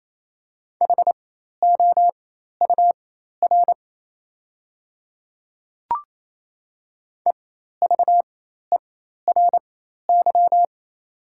Morse code, heard as a keyed tone, about 28 wpm, spaced as for 13 wpm, 690 Hz. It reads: HOUR EVERY
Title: 4x Character Spacing - Top 500 Words - 28wpm